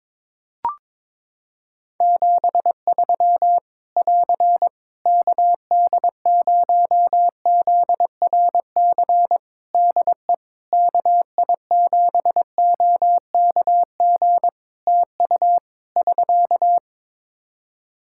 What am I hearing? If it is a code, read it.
Morse code, heard as a keyed tone, about 22 wpm, 700 Hz. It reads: 73 <AR> KD0ZRC DE KI7OKG TU <SK>
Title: K1USN SST QSO Practice 22wpm